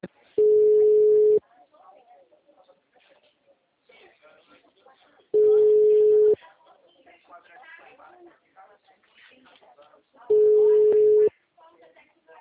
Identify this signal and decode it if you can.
Ringback tone (Europe, 425 Hz), with speech.